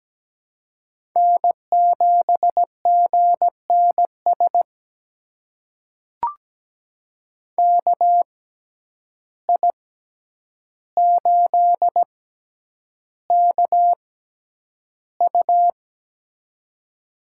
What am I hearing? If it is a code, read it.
Morse code, heard as a keyed tone, about 17 wpm, spaced as for 9 wpm, 700 Hz. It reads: N7GNS KI8KU
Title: USA Call Signs - 5 Characters - 6x Character Spacing - 17wpm